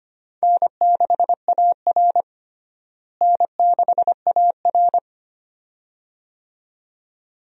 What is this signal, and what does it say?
Morse code, heard as a keyed tone, about 25 wpm, 700 Hz. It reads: N6AR N6AR
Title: USA Call Signs - 4 Characters - Repeated 2x - 25wpm